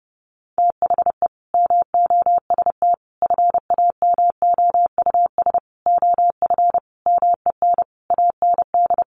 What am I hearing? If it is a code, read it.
Morse code, heard as a keyed tone, about 30 wpm, 700 Hz. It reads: THE MOST FAMOUS OF MEN AND
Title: Sentences from Top 1500 Words 30wpm